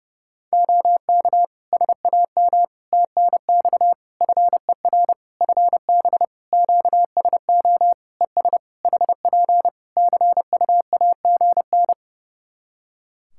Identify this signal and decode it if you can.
Morse code, heard as a keyed tone, about 30 wpm, 700 Hz. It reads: OK SAM TNX FER FB QSO ES HP CUAGN
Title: Practice QSO - Exchange 4 - Ending - 30wpm